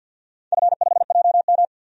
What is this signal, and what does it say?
Morse code, heard as a keyed tone, about 46 wpm, 700 Hz. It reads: F5JM
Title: DXCC-100 Call Signs - 4 Characters - 50wpm